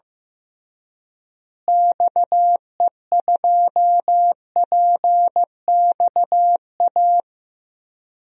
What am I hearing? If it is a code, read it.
Morse code, heard as a keyed tone, about 15 wpm, 700 Hz. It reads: XE2PXA
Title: DXCC-100 Call Signs - 6 Characters - Repeated 3x - 15wpm